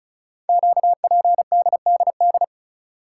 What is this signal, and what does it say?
Morse code, heard as a keyed tone, about 36 wpm, 700 Hz. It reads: QPDDD
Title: Code Groups - 5 Letters - 35wpm